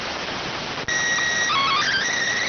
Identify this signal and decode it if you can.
SSTV (slow-scan television), PD 290.